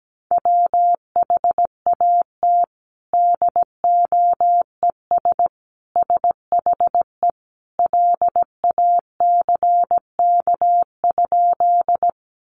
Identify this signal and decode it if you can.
Morse code, heard as a keyed tone, about 17 wpm, 700 Hz. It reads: WHAT DOES SHE LACK?
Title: Sentences from Top 1100 Words 17wpm